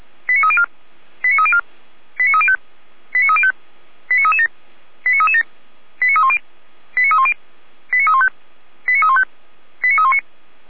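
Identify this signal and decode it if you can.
Five-tone selective call (ZVEI), heard as quick runs of five tones, tones of about 70 ms. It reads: 89304 89304 89305 89305 89337 89337 89310 89310 89315 89315 89319
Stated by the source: Five-tone ZVEI selective call received by SDR, fire brigade Feuerwehr Orlamünde.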